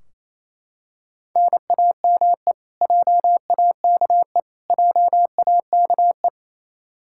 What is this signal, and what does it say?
Morse code, heard as a keyed tone, about 28 wpm, 700 Hz. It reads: NAME JAKE JAKE
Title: Practice QSO - Exchange 1 - Beginning - 28wpm